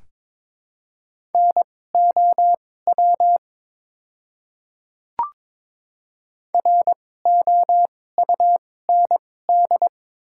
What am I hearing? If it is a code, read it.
Morse code, heard as a keyed tone, about 22 wpm, spaced as for 16 wpm, 700 Hz. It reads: NOW ROUND
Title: 2x Character Spacing - Top 200 Words - 22wpm